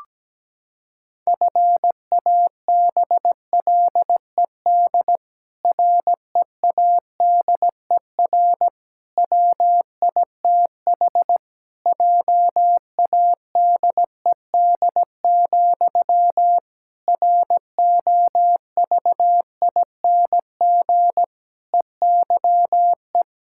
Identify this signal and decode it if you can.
Morse code, heard as a keyed tone, about 17 wpm, 700 Hz. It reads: FABLED READER WITH JADED, ROVING EYE
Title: Pangram Challenge v2 - 17wpm